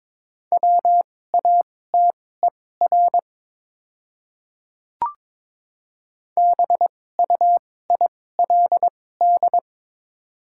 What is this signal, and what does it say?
Morse code, heard as a keyed tone, about 21 wpm, spaced as for 16 wpm, 700 Hz. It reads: WATER BUILD
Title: 2x Character Spacing - Top 200 Words - 22wpm